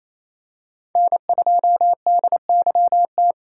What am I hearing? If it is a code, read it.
Morse code, heard as a keyed tone, about 28 wpm, 700 Hz. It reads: N2DYT
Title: K1USN SST QSO Practice - 28wpm